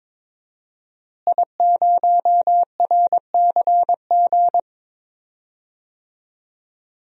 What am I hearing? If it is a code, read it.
Morse code, heard as a keyed tone, about 22 wpm, 700 Hz. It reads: I0RCG
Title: DXCC-100 Call Signs - 5 Characters - Repeated 2x - 22wpm